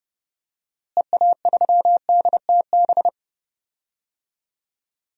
Morse code, heard as a keyed tone, about 31 wpm, 700 Hz. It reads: EA3DTB
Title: DXCC-100 Call Signs - 6 Characters - Repeated 3x - 30wpm